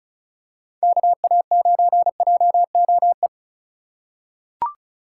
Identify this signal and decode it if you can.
Morse code, heard as a keyed tone, about 34 wpm, 700 Hz. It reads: KA9JOE
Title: USA Call Signs - 6 Characters - 4x Character Spacing - 35wpm